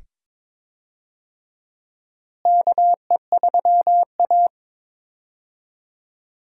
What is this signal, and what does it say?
Morse code, heard as a keyed tone, about 22 wpm, 700 Hz. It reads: KE3A
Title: USA Call Signs - 4 Characters - Repeated 2x - 22wpm